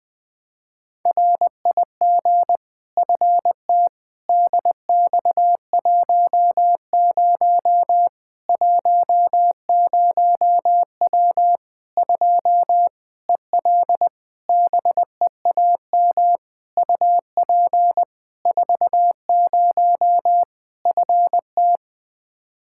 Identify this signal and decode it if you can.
Morse code, heard as a keyed tone, about 21 wpm, 700 Hz. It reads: RIG FT DX10 10W 2 EL BEAM UP 40 FT
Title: Practice QSO - Exchange 2 - Info - 20wpm